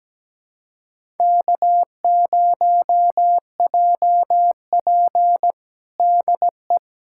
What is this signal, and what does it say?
Morse code, heard as a keyed tone, about 17 wpm, 700 Hz. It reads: K0JP DE
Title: USA Callsign DE Callsign - 4 Characters - 17wpm